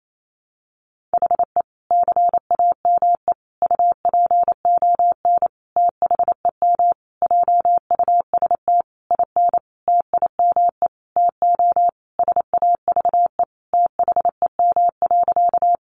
Morse code, heard as a keyed tone, about 28 wpm, 700 Hz. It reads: HE CAME UPON THEM JUST IN TIME TO SAVE THEM.
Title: Sentences from Top 1100 Words 28wpm